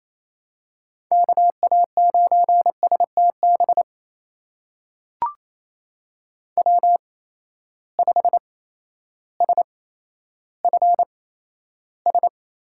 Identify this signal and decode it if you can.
Morse code, heard as a keyed tone, about 28 wpm, spaced as for 13 wpm, 700 Hz. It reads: KA9STB W5SFS
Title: USA Call Signs - 8x Character Spacing - 28wpm